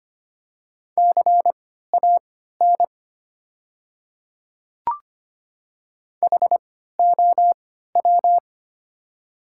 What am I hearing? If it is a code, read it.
Morse code, heard as a keyed tone, about 25 wpm, spaced as for 14 wpm, 700 Hz. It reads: CAN HOW
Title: 3x Character Spacing - Top 100 Words - 25wpm